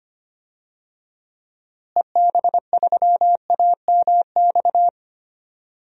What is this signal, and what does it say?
Morse code, heard as a keyed tone, about 25 wpm, 700 Hz. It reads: EB3AMX
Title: DXCC-100 Call Signs - 6 Characters - Repeated 3x - 25wpm